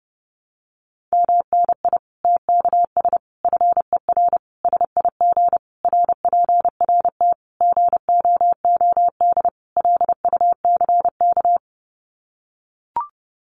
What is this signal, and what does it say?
Morse code, heard as a keyed tone, about 30 wpm, 700 Hz. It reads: MNI TKS FER SIG RPRT GOOD LUCK
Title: Common QSO Phrases 30wpm